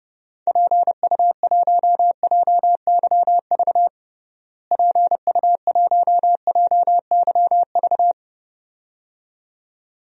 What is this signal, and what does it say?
Morse code, heard as a keyed tone, about 31 wpm, 700 Hz. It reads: PU1JYV PU1JYV
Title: DXCC-100 Call Signs - 6 Characters - Repeated 2x - 30wpm